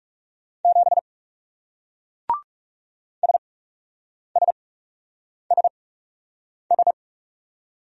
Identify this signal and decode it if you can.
Morse code, heard as a keyed tone, about 38 wpm, spaced as for 8 wpm, 700 Hz. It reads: 7 SSSS